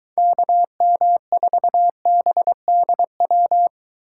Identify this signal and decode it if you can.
Morse code, heard as a keyed tone, about 23 wpm, 700 Hz. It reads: KM4BDW